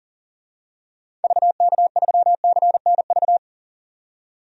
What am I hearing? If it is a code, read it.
Morse code, heard as a keyed tone, about 40 wpm, 700 Hz. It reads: VK3CNV
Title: DXCC-100 Call Signs - 6 Characters - 40wpm